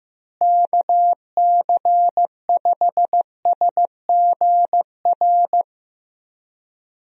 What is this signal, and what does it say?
Morse code, heard as a keyed tone, about 15 wpm, 700 Hz. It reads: KC5SGR
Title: USA Call Signs - 6 Characters - Repeated 3x - 15wpm